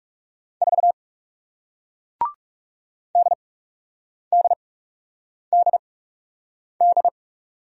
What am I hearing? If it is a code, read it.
Morse code, heard as a keyed tone, about 39 wpm, spaced as for 8 wpm, 700 Hz. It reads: 4 DDDD